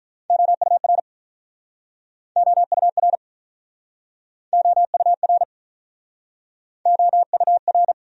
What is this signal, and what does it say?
Morse code, heard as a keyed tone, about 48 wpm, 700 Hz. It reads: OUR OUR OUR OUR